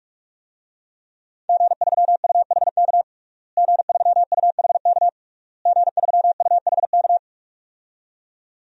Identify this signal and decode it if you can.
Morse code, heard as a keyed tone, about 46 wpm, 700 Hz. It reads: G3UHK G3UHK G3UHK
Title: DXCC-100 Call Signs - 5 Characters - Repeated 3x - 45wpm